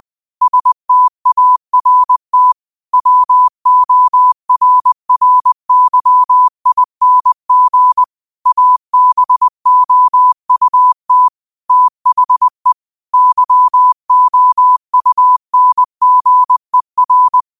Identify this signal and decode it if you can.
Morse code, heard as a keyed tone, about 20 wpm, 1000 Hz. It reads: START WORRYING ABOUT THE YOUNGER
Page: Daily quotes in Morse Code at 20 Words per Minute. Use this to improve your code copying skill.